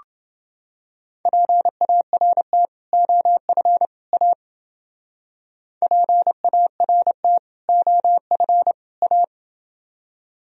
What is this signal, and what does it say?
Morse code, heard as a keyed tone, about 27 wpm, 700 Hz. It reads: PART OF A PART OF A